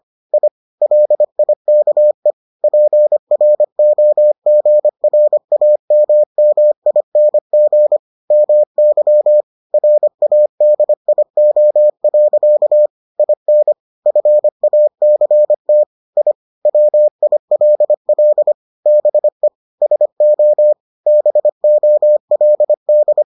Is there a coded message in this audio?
Yes: Morse code from 0.3 to 23.2 s, heard as a keyed tone, about 25 wpm, 600 Hz. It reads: I LIKE PROGRAMMING MY RADIO. IN FACT I WILL BE SO BOLD